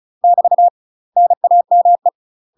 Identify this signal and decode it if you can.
Morse code, heard as a keyed tone, about 36 wpm, 700 Hz. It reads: <BT> NAME
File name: UK MASTER QSOs TRAINING/35WPM